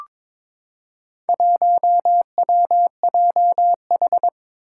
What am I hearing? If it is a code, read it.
Morse code, heard as a keyed tone, about 22 wpm, 700 Hz. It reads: 1WJH
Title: Four Challenging Letter-Number Combinations 22wpm